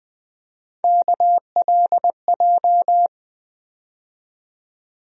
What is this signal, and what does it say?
Morse code, heard as a keyed tone, about 20 wpm, 700 Hz. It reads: KLJ
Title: USA Call Sign Postfixes - 3 Letters 20wpm